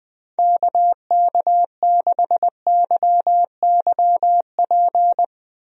Morse code, heard as a keyed tone, about 20 wpm, 700 Hz. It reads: KK6YYP
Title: USA Call Signs - 6 Characters - Repeated 2x - 20wpm